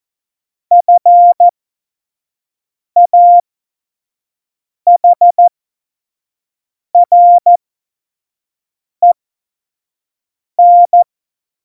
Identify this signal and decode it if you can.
Morse code, heard as a keyed tone, about 14 wpm, spaced as for 5 wpm, 700 Hz. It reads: FAHREN
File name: Morsekurs / Witze / Witz 19 Mai 2020, 25 F70